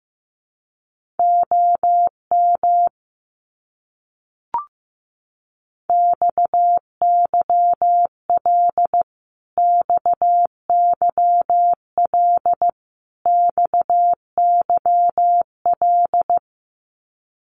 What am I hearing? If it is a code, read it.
Morse code, heard as a keyed tone, about 15 wpm, 700 Hz. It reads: OM XYL XYL XYL
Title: Abbreviations Repeated 3x - 15wpm